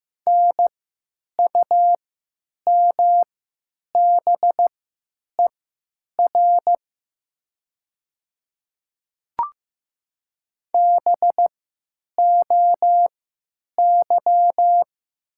Morse code, heard as a keyed tone, about 15 wpm, spaced as for 9 wpm, 700 Hz. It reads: NUMBER BOY
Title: 3x Character Spacing - Top 200 Words - 15wpm